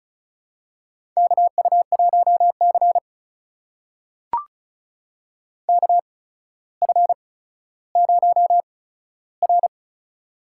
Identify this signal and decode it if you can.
Morse code, heard as a keyed tone, about 36 wpm, spaced as for 15 wpm, 700 Hz. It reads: KU1C KF0R
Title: USA Call Signs - 4 Characters - 8x Character Spacing - 35wpm